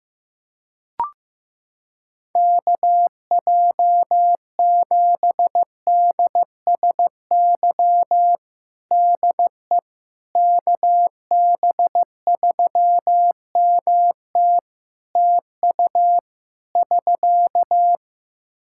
Morse code, heard as a keyed tone, about 15 wpm, 700 Hz. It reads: KJ7DSY DE KB3MT TU <SK>
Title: K1USN SST QSO Practice 15wpm